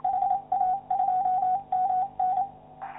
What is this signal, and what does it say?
Morse code, heard as a keyed tone, about 27 wpm, 740 Hz, with radio noise. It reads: HA2MN
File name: QSO clips / YI9PSE-20m- -0519Z